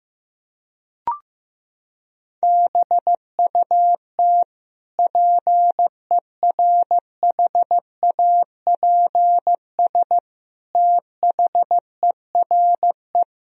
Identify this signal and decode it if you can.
Morse code, heard as a keyed tone, about 15 wpm, 700 Hz. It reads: BUT PERHAPS THERE
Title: Sentences from Top 1000 Words - Review Part 3 - 15wpm